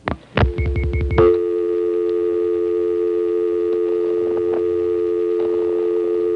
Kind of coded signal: telephone call progress tone